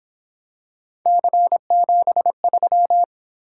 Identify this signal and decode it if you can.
Morse code, heard as a keyed tone, about 27 wpm, 700 Hz. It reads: C73
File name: Speed-Racing - Three Letter-Number Combinations - 22wpm